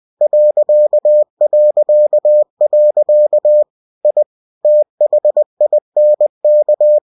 Morse code, heard as a keyed tone, about 20 wpm, 600 Hz. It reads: ... I THINK
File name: cw-books / winnie-the-pooh / winnie-20wpm-page-0038